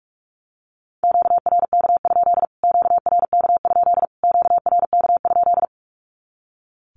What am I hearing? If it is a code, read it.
Morse code, heard as a keyed tone, about 44 wpm, 700 Hz. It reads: QRK? QRK? QRK?
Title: Q-Codes Repeated 3x 45wpm